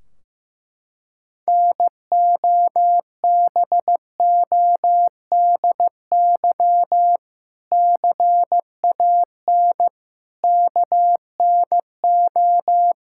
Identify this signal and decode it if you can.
Morse code, heard as a keyed tone, about 15 wpm, 700 Hz. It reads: NOBODY CAN KNO
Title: Sentences from Top 1600 Words 15wpm